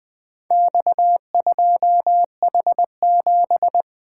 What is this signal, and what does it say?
Morse code, heard as a keyed tone, about 20 wpm, 700 Hz. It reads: X2H7
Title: Four Letter-Number Combinations - 20wpm